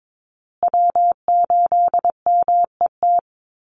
Morse code, heard as a keyed tone, about 22 wpm, 700 Hz. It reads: W8MET
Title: Call Signs Repeated 2x - 22wpm